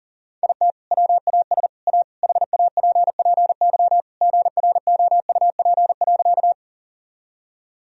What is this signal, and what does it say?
Morse code, heard as a keyed tone, about 40 wpm, 700 Hz. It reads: IT WAS A HAPPY GROUP.
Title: Sentences from Top 1000 Words - Review Part 1 - 40wpm